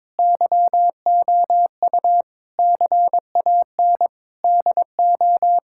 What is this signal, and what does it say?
Morse code, heard as a keyed tone, about 22 wpm, 700 Hz. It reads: YOU CAN DO
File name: Difficult Sentences with Hints - 22wpm